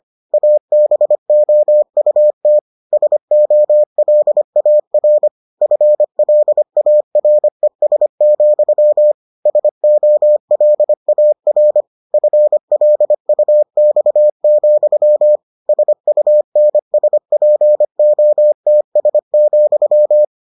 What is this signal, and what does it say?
Morse code, heard as a keyed tone, about 25 wpm, 600 Hz. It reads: ABOUT SOLAR FLARES, SOLAR FLUX, SUNSPOTS,